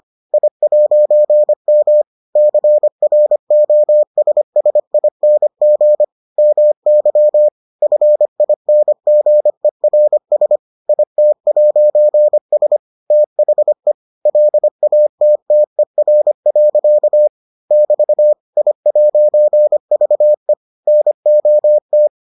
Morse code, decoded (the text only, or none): I'M CROSSING MY FINGERS IT'S THE LATTER. <BT> I'VE NOT